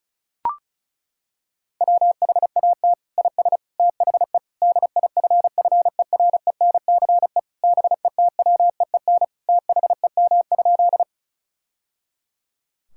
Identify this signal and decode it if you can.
Morse code, heard as a keyed tone, about 36 wpm, 700 Hz. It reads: WHAT IS THE DIFFERENCE BETWEEN THEM?
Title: Sentences from Top 1000 Words - Review Part 5 - 35wpm